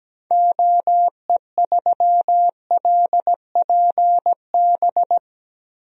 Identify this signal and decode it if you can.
Morse code, heard as a keyed tone, about 17 wpm, 700 Hz. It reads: OE3LPB
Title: DXCC-100 Call Signs - 6 Characters - Repeated 2x - 17wpm